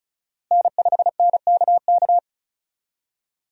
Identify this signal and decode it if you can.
Morse code, heard as a keyed tone, about 34 wpm, 700 Hz. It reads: N5NKK
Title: USA Call Signs - 5 Characters - Repeated 2x - 35wpm